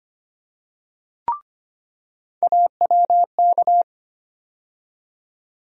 Morse code, heard as a keyed tone, about 25 wpm, 700 Hz. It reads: AWK
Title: USA Call Sign Postfixes - 3 Letters 25wpm